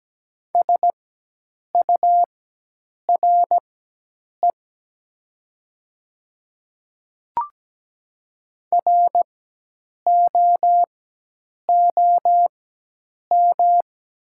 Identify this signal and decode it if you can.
Morse code, heard as a keyed tone, about 17 wpm, spaced as for 8 wpm, 700 Hz. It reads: SURE ROOM
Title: 4x Character Spacing - Top 300 Words - 17wpm